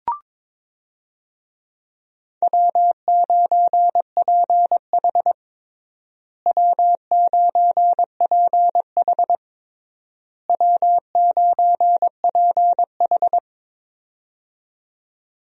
Morse code, heard as a keyed tone, about 22 wpm, 700 Hz. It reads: W9PH W9PH W9PH
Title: USA Call Signs - 4 Characters - Repeated 3x - 22wpm